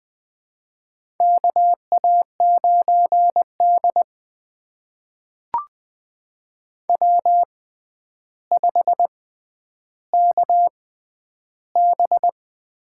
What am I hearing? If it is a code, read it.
Morse code, heard as a keyed tone, about 20 wpm, spaced as for 10 wpm, 700 Hz. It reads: KA9D W5KB
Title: USA Call Signs - 4 Characters - 6x Character Spacing - 20wpm